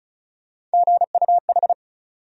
Morse code, heard as a keyed tone, about 34 wpm, 700 Hz. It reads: GUH
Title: USA Call Sign Postfixes - 3 Letters 35wpm